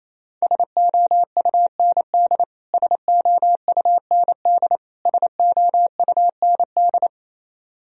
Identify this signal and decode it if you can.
Morse code, heard as a keyed tone, about 28 wpm, 700 Hz. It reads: SOUND SOUND SOUND
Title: Top 100 Words Repeated 28wpm